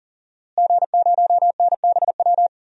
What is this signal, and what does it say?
Morse code, heard as a keyed tone, about 40 wpm, 700 Hz. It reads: G0NBW
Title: DXCC-100 Call Signs - 5 Characters - Repeated 3x - 40wpm